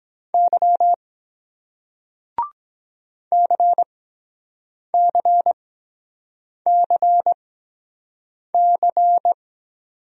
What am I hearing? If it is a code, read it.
Morse code, heard as a keyed tone, about 23 wpm, spaced as for 7 wpm, 700 Hz. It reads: Y CCCC